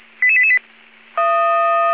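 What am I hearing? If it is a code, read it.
Five-tone selective call (ZVEI), heard as quick runs of five tones, tones of about 70 ms. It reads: 88888
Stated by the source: ZVEI five-tone call 55555 received on SDR.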